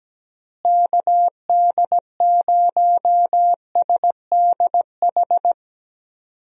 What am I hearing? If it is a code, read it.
Morse code, heard as a keyed tone, about 17 wpm, 700 Hz. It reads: KD0SDH
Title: USA Call Signs - 6 Characters - Repeated 3x - 17wpm